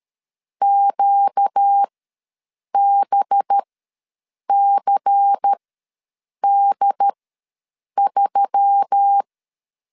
Morse code, heard as a keyed tone, about 13 wpm, spaced as for 7 wpm, 800 Hz. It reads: QBCD3
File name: LEZIONI / 10 SIGNS / 10A SLASH ws 12wpm 3443